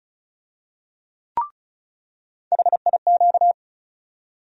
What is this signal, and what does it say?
Morse code, heard as a keyed tone, about 34 wpm, 700 Hz. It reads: HIQ
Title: USA Call Sign Postfixes - 3 Letters 35wpm